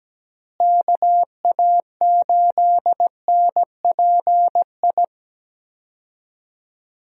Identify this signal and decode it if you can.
Morse code, heard as a keyed tone, about 17 wpm, 700 Hz. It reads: KA8NPI
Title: USA Call Signs - 6 Characters - Repeated 2x - 17wpm